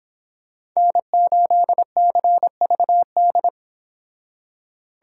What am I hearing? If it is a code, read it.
Morse code, heard as a keyed tone, about 25 wpm, 700 Hz. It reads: N8CVD